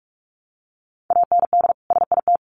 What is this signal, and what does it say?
Morse code, heard as a keyed tone, about 44 wpm, 700 Hz. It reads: AND SIT
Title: Sentences from Top 500 Words 45wpm